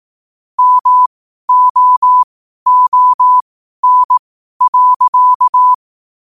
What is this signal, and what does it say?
Morse code, heard as a keyed tone, about 18 wpm, spaced as for 12 wpm, 1000 Hz. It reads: MOON.
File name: qotdfiles / 12wpm